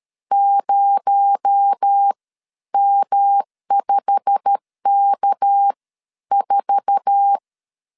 Morse code, heard as a keyed tone, about 13 wpm, spaced as for 10 wpm, 800 Hz. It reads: 0M5K4
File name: CW / DET / 1 9 1500C S A 12wpm 3800